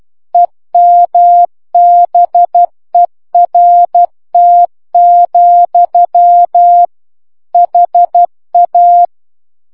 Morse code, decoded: EMBERT, HA